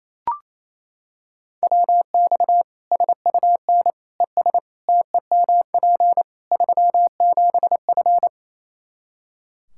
Morse code, heard as a keyed tone, about 28 wpm, 700 Hz. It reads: WX SUN ES TEMP 37F
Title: K1USN SST QSO Practice 28wpm